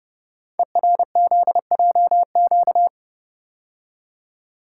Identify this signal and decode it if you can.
Morse code, heard as a keyed tone, about 28 wpm, 690 Hz. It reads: ERZJQ
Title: Code Groups - 5 Letters - 30wpm